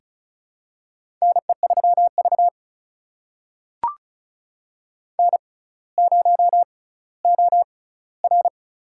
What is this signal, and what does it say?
Morse code, heard as a keyed tone, about 34 wpm, spaced as for 18 wpm, 700 Hz. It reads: NE3V N0OR
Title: USA Call Signs - 4 Characters - 6x Character Spacing - 35wpm